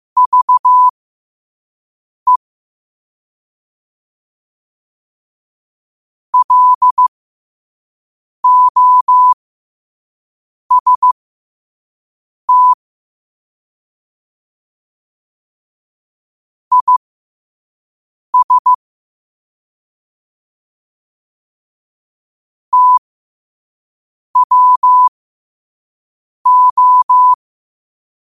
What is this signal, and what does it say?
Morse code, heard as a keyed tone, about 15 wpm, spaced as for 5 wpm, 1000 Hz. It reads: VE LOST IS TWO
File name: qotdfiles / 5wpm